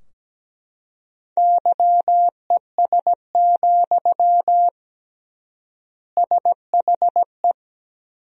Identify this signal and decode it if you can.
Morse code, heard as a keyed tone, about 17 wpm, 700 Hz. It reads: YES, SHE
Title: 3x Word-Spacing - Sentences from Top 500 Words - 17wpm